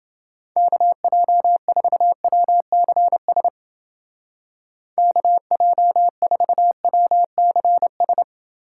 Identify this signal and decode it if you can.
Morse code, heard as a keyed tone, about 30 wpm, 700 Hz. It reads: KJ4WCS KJ4WCS